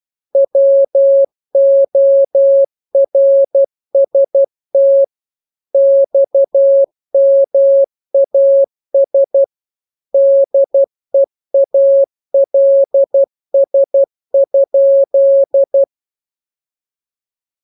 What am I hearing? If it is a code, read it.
Morse code, heard as a keyed tone, about 12 wpm, 550 Hz. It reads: WORST XMAS DEALS?